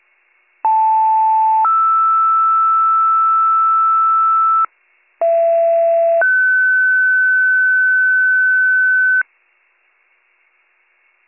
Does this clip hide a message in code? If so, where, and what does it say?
Yes: two-tone page from 0.6 to 9.2 s, heard as a one-second tone, then a three-second tone, about 880 then 1320 Hz, about 670 then 1500 Hz.